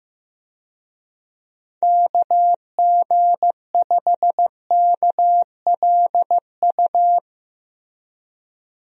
Morse code, heard as a keyed tone, about 15 wpm, 700 Hz. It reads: KG5KLU
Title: USA Call Signs - 6 Characters - Repeated 2x - 15wpm